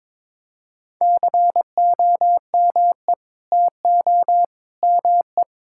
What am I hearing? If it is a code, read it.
Morse code, heard as a keyed tone, about 22 wpm, 700 Hz. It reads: COME TO ME
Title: Sentences from Top 1000 Words - Review Part 6 - 22wpm